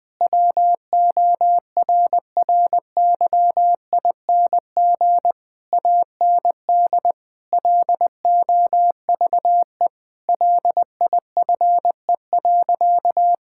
Morse code, heard as a keyed tone, about 20 wpm, 700 Hz. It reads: WORRYING AND LOVE LIFE.
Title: Sentences with 8-Word Phrases - 20wpm